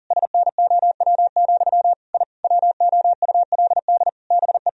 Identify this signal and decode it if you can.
Morse code, heard as a keyed tone, about 40 wpm, 700 Hz. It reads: SNOW, I WOULD BE
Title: Sentences from Top 1000 Words - Review Part 3 - 40wpm